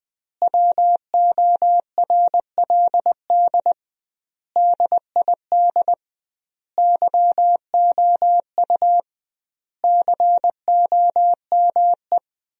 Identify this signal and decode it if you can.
Morse code, heard as a keyed tone, about 21 wpm, 700 Hz. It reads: WORLD DID YOU COME
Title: 2x Word-Spacing - Sentences from Top 200 Words - 20wpm